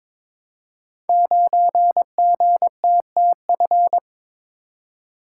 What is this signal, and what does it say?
Morse code, heard as a keyed tone, about 22 wpm, 700 Hz. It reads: 9GTTF
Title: Code Groups - 5 Letter:Numbers - 22wpm